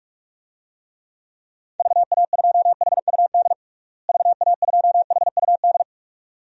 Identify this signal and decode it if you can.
Morse code, heard as a keyed tone, about 46 wpm, 700 Hz. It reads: VA2HUD VA2HUD
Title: DXCC-100 Call Signs - 6 Characters - Repeated 2x - 45wpm